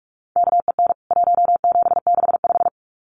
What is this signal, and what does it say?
Morse code, heard as a keyed tone, about 44 wpm, 700 Hz. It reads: KEN 1765
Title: CWT Exchange 45wpm